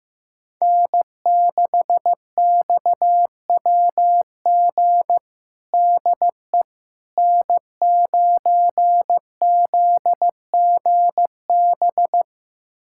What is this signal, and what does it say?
Morse code, heard as a keyed tone, about 15 wpm, 700 Hz. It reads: N6XWG DE N9ZGB
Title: USA Callsign DE Callsign - 5 Characters - 15wpm